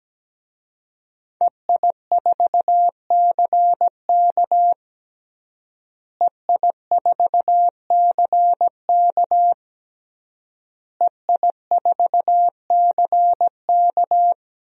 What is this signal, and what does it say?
Morse code, heard as a keyed tone, about 17 wpm, 700 Hz. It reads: EI4CK EI4CK EI4CK
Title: DXCC-100 Call Signs - 5 Characters - Repeated 3x - 17wpm